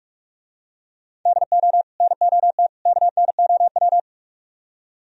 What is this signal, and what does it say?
Morse code, heard as a keyed tone, about 44 wpm, 700 Hz. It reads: DO NOT KNOW